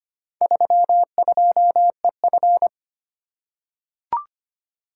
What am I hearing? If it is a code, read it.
Morse code, heard as a keyed tone, about 25 wpm, 700 Hz. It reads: 32EF
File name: practice / 25wpm / Four Letter-Number Combinations - 25wpm